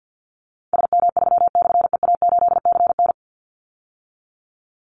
Morse code, heard as a keyed tone, about 50 wpm, 700 Hz. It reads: SM3/EA8CN
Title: Call Signs - 50wpm